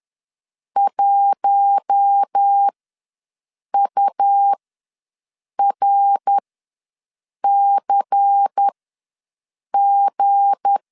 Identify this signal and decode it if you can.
Morse code, heard as a keyed tone, about 10 wpm, spaced as for 6 wpm, 800 Hz. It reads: 1URCG
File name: LEZIONI / 9 28 ws 10wpm 4124